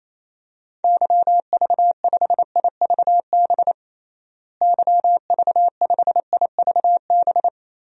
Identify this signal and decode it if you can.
Morse code, heard as a keyed tone, about 28 wpm, 700 Hz. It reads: YV5IVB YV5IVB
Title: DXCC-100 Call Signs - 6 Characters - Repeated 3x - 28wpm